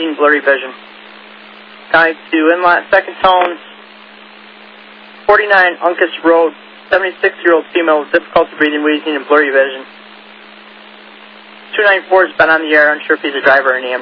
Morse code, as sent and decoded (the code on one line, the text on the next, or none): none